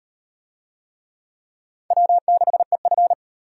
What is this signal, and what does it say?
Morse code, heard as a keyed tone, about 37 wpm, 700 Hz. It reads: W6EF